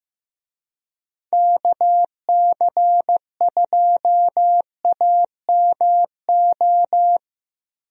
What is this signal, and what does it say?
Morse code, heard as a keyed tone, about 15 wpm, 700 Hz. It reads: KC2AMO
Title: USA Call Signs - 6 Characters - Repeated 2x - 15wpm